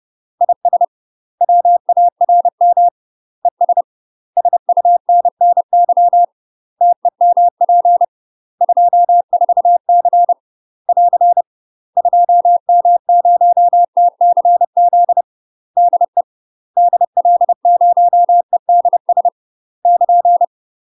Morse code, heard as a keyed tone, about 30 wpm, 700 Hz. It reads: IS WARM ES SUNNY TEMP 24C <AR> 2M0TCZ DE DL0EDS <KN>
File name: UK MASTER QSOs TRAINING / 30WPM / UK MASTER QSOs 0048